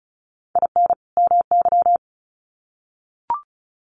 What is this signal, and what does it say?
Morse code, heard as a keyed tone, about 34 wpm, 700 Hz. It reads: IN MY
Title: Sets of 2 Words 35wpm